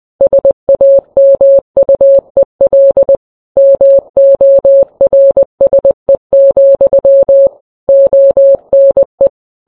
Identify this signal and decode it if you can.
Morse code, heard as a keyed tone, about 20 wpm, 550 Hz. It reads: SAMUEL MORSE, ONE